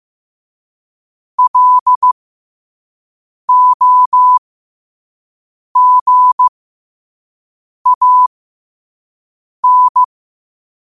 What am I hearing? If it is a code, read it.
Morse code, heard as a keyed tone, about 15 wpm, spaced as for 5 wpm, 1000 Hz. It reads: LOGAN
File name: qotdfiles / 5wpm